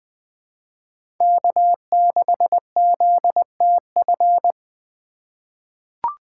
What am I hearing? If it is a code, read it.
Morse code, heard as a keyed tone, about 20 wpm, 700 Hz. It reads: K6ZTF
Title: USA Call Signs - 5 Characters - 2x Character Spacing - 20wpm